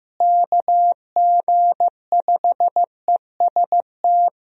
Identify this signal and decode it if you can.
Morse code, heard as a keyed tone, about 16 wpm, 700 Hz. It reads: KG5EST
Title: USA Call Signs - 6 Characters - Repeated 3x - 15wpm